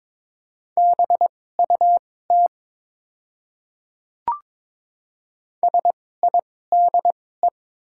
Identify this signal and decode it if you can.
Morse code, heard as a keyed tone, about 22 wpm, spaced as for 16 wpm, 700 Hz. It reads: BUT SIDE
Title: 2x Character Spacing - Top 100 Words - 22wpm